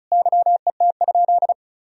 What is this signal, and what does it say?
Morse code, heard as a keyed tone, about 36 wpm, 700 Hz. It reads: YET?
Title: Sentences from Top 1000 Words - Review Part 2 - 35wpm